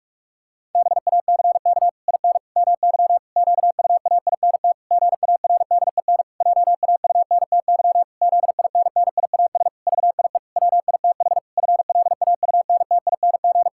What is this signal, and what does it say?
Morse code, heard as a keyed tone, about 46 wpm, 700 Hz. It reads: BACK IN MY QUAINT GARDEN JAUNTY ZINNIAS VIE WITH FLAUNTING
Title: Pangram Challenge v2 - 45wpm